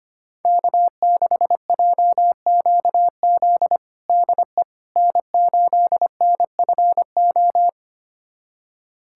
Morse code, heard as a keyed tone, about 25 wpm, 700 Hz. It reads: K6JQZ DE N8NFO
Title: USA Callsign DE Callsign - 5 Characters - 25wpm